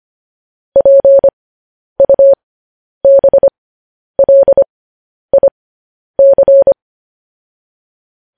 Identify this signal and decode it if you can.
Morse code, heard as a keyed tone, about 25 wpm, spaced as for 10 wpm, 550 Hz. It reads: PUBLIC